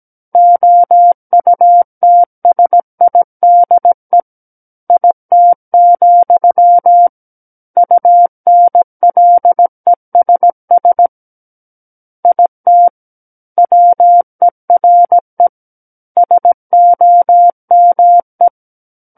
Morse code, heard as a keyed tone, about 17 wpm, 700 Hz. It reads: OUTSIDE IT, UNLESS IT WERE SOME
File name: Book One / 08-Chapter 8 - 17 WPM